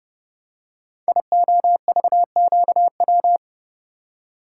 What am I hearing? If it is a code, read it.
Morse code, heard as a keyed tone, about 30 wpm, 700 Hz. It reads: IOVQW
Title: Code Groups - 5 Letter:Numbers - 30wpm